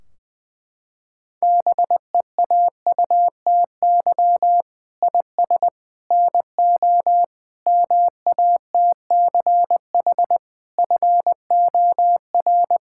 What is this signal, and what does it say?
Morse code, heard as a keyed tone, about 20 wpm, 700 Hz. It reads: BEAUTY IS NO MATCH FOR
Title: Sentences from Top 1600 Words 20wpm